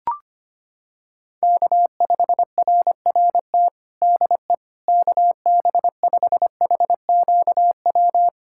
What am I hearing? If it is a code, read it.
Morse code, heard as a keyed tone, about 25 wpm, 700 Hz. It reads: K5RRT DE KB5HQW
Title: USA Callsign DE Callsign 25wpm